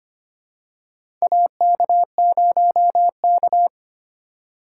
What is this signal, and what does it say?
Morse code, heard as a keyed tone, about 25 wpm, 700 Hz. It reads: AK0K
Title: USA Call Signs - 4 Characters - 3x Character Spacing - 25wpm